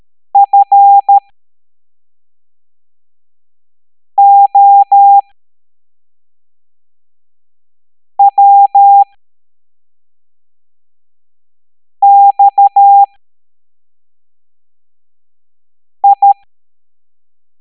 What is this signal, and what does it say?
Morse code, heard as a keyed tone, about 13 wpm, spaced as for 3 wpm, 800 Hz. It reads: FOWXI